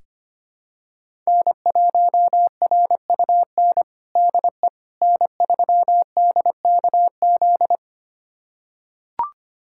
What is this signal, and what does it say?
Morse code, heard as a keyed tone, about 25 wpm, 700 Hz. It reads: N1RUN DE N3DKZ
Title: USA Callsign DE Callsign - 5 Characters - 25wpm